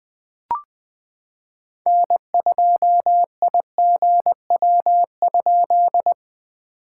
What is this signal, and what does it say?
Morse code, heard as a keyed tone, about 20 wpm, 700 Hz. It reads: N2IGW?
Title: POTA QSO Practice - 20wpm